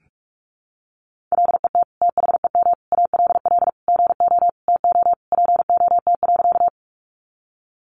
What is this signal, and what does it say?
Morse code, heard as a keyed tone, about 44 wpm, 700 Hz. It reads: LET THEM ALL GO TO POT.
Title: Sentences from Top 1000 Words 45wpm